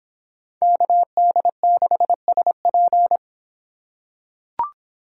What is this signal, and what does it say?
Morse code, heard as a keyed tone, about 24 wpm, 700 Hz. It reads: KD6SP